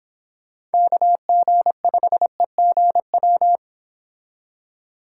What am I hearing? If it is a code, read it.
Morse code, heard as a keyed tone, about 28 wpm, 700 Hz. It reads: KG5EGW